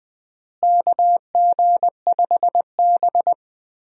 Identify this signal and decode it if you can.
Morse code, heard as a keyed tone, about 20 wpm, 700 Hz. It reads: KG5B
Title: USA Call Signs - 4 Characters - Repeated 2x - 20wpm